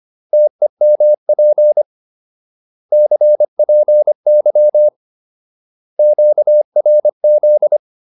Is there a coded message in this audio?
Yes: Morse code from 0.3 to 7.8 s, heard as a keyed tone, about 25 wpm, 600 Hz. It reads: TEMP CPY QRZ